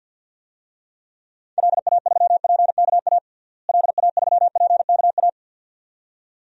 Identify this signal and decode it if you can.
Morse code, heard as a keyed tone, about 49 wpm, 700 Hz. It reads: LA3PKA LA3PKA
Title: DXCC-100 Call Signs - 6 Characters - Repeated 2x - 50wpm